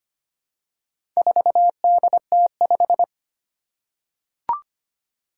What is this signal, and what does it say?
Morse code, heard as a keyed tone, about 25 wpm, 700 Hz. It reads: 4DT5
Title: Four Challenging Letter-Number Combinations 25wpm